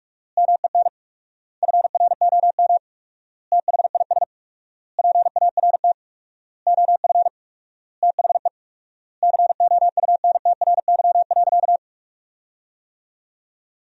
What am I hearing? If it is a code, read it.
Morse code, heard as a keyed tone, about 46 wpm, 700 Hz. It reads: MEN FROM THIS PART OF THE COUNTRY.
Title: 4x Word-Spacing - Sentences from Top 300 Words - 45wpm